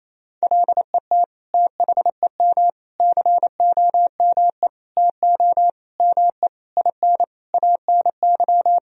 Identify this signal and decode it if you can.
Morse code, heard as a keyed tone, about 28 wpm, 700 Hz. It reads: LET THEM COME TO ME IN ANY
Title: Sentences from Top 1000 Words - Review Part 3 - 28wpm